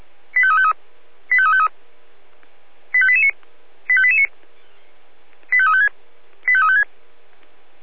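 Five-tone selective call (ZVEI), heard as quick runs of five tones, tones of about 70 ms. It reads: 85353 85353 85889 85889 85356 85356
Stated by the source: ZVEI five-tone selective call received on an SDR.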